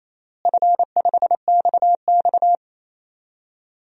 Morse code, heard as a keyed tone, about 28 wpm, 700 Hz. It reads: F5XX
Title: DXCC-100 Call Signs - 4 Characters - Repeated 3x - 28wpm